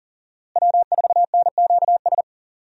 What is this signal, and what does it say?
Morse code, heard as a keyed tone, about 40 wpm, 700 Hz. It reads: W4NQS
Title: USA Call Signs - 5 Characters - Repeated 2x - 40wpm